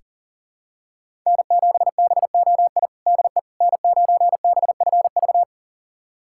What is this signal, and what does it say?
Morse code, heard as a keyed tone, about 37 wpm, 700 Hz. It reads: N7BOI DE N9BFV
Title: USA Callsign DE Callsign - 5 Characters - 40wpm